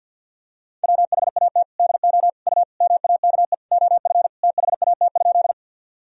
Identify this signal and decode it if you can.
Morse code, heard as a keyed tone, about 50 wpm, 700 Hz. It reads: WHAT DO U MAKE OF THAT?
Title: Sentences from Top 100 Words and Abbreviations 50wpm